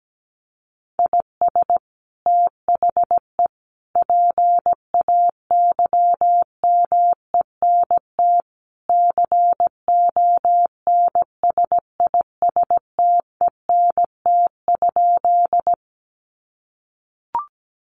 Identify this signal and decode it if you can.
Morse code, heard as a keyed tone, about 17 wpm, 700 Hz. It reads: IS THE PAYMENT CONSISTENT?
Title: Sentences from Top 1500 Words 17wpm